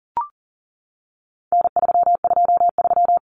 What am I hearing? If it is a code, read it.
Morse code, heard as a keyed tone, about 40 wpm, 700 Hz. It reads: N323
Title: Common Cut Numbers - 4-Digit 40wpm